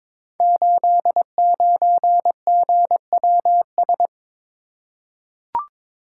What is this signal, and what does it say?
Morse code, heard as a keyed tone, about 22 wpm, 700 Hz. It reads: 89GWS
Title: Code Groups - 5 Letter:Numbers - 22wpm